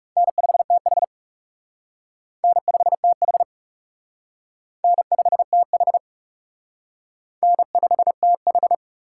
Morse code, heard as a keyed tone, about 38 wpm, 700 Hz. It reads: N5TH N5TH N5TH N5TH
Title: Speed Racing - USA Call Signs 30wpm